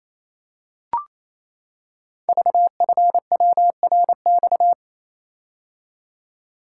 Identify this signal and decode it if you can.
Morse code, heard as a keyed tone, about 27 wpm, 700 Hz. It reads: VFWRX
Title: Code Groups - 5 Letters - 28wpm